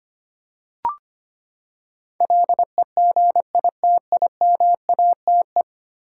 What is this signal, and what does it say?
Morse code, heard as a keyed tone, about 25 wpm, 700 Hz. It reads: LEGITIMATE
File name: Ten Letter Words - 25wpm